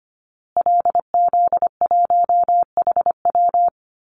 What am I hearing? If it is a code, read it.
Morse code, heard as a keyed tone, about 25 wpm, 700 Hz. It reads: LZ1HW
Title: Call Signs Repeated 2x - 25wpm